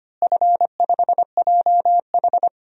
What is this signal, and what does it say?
Morse code, heard as a keyed tone, about 25 wpm, 700 Hz. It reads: F5JH
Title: DXCC-100 Call Signs - 4 Characters - Repeated 2x - 25wpm